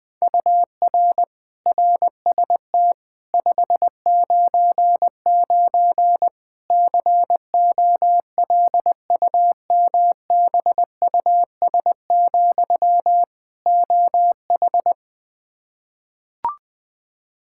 Morse code, heard as a keyed tone, about 20 wpm, 700 Hz. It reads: UR RST 599 COLUMBUS, OH
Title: K1USN SST QSO Practice 20wpm